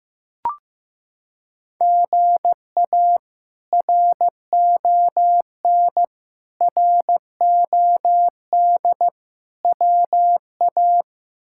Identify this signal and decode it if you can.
Morse code, heard as a keyed tone, about 16 wpm, 700 Hz. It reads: GA RON ROD WA
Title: K1USN SST QSO Practice - 15wpm